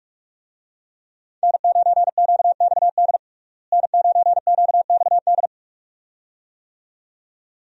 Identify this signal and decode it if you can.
Morse code, heard as a keyed tone, about 46 wpm, 700 Hz. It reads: N9QXD N9QXD
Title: USA Call Signs - 5 Characters - Repeated 2x - 45wpm